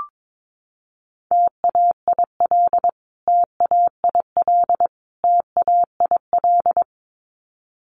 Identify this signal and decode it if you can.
Morse code, heard as a keyed tone, about 22 wpm, 700 Hz. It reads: TAIL TAIL TAIL
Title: Top 500 Words Repeated 22wpm